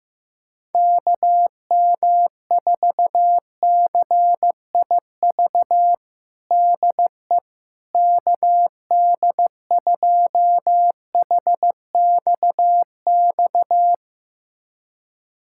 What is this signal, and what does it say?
Morse code, heard as a keyed tone, about 16 wpm, 700 Hz. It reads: KM4CIV DE KD2HXX
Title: USA Callsign DE Callsign 15wpm